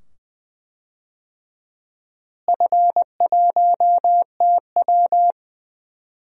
Morse code, heard as a keyed tone, about 21 wpm, 700 Hz. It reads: F1TW
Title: DXCC-100 Call Signs - 4 Characters - Repeated 2x - 20wpm